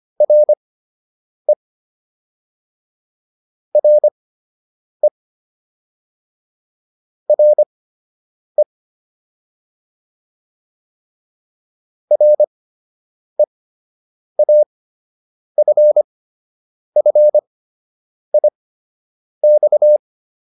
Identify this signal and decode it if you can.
Morse code, heard as a keyed tone, about 25 wpm, spaced as for 8 wpm, 600 Hz. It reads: RE RE RE REAFFIX